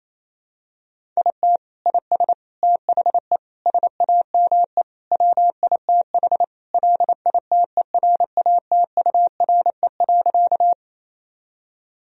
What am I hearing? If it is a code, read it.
Morse code, heard as a keyed tone, about 27 wpm, 700 Hz. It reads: IT IS THE SAME WITH LITERATURE.
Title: Sentences from Top 1000 Words - Review Part 3 - 28wpm